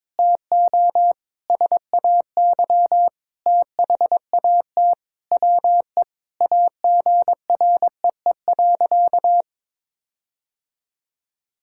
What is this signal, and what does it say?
Morse code, heard as a keyed tone, about 22 wpm, 700 Hz. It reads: TO SAY THAT WE AGREE.
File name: practice / 22wpm / Sentences with 8-Word Phrases - 22wpm